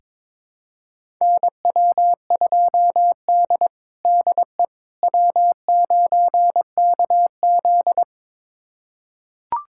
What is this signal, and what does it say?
Morse code, heard as a keyed tone, about 22 wpm, 700 Hz. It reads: NW2D DE W9KZ
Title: USA Callsign DE Callsign - 4 Characters - 22wpm